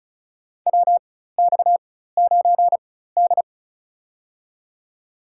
Morse code, heard as a keyed tone, about 35 wpm, spaced as for 16 wpm, 700 Hz. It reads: WX9D
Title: USA Call Signs - 4x Character Spacing - 35wpm